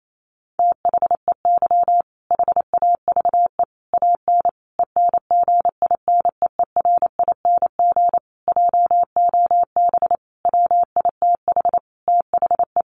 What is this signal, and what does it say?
Morse code, heard as a keyed tone, about 28 wpm, 700 Hz. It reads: THEY HAVE AN ENGINEERING JOB WITH THE
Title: Sentences from Top 1500 Words 28wpm